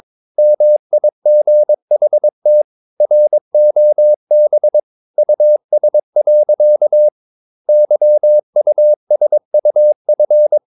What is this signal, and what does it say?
Morse code, heard as a keyed tone, about 22 wpm, 600 Hz. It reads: MIGHT ROB US. YUSUF
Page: Romanen Never af Ken Follet på engelsk.